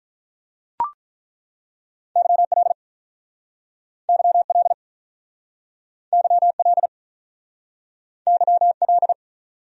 Morse code, heard as a keyed tone, about 48 wpm, 700 Hz. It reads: YL YL YL YL